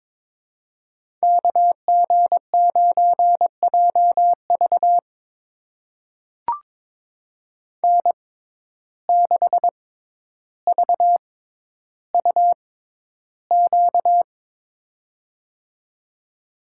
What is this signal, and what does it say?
Morse code, heard as a keyed tone, about 23 wpm, spaced as for 11 wpm, 700 Hz. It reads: KG9JV N6VUQ
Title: USA Call Signs - 5 Characters - 6x Character Spacing - 22wpm